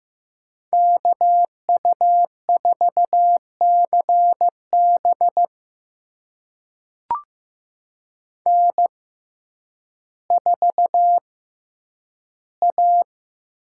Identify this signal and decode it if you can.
Morse code, heard as a keyed tone, about 15 wpm, spaced as for 9 wpm, 700 Hz. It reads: KU4CB N4A
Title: USA Call Signs - 5 Characters - 6x Character Spacing - 15wpm